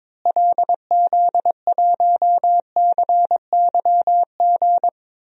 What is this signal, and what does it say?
Morse code, heard as a keyed tone, about 22 wpm, 700 Hz. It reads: LZ1CYG
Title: DXCC-100 Call Signs - 6 Characters - Repeated 2x - 22wpm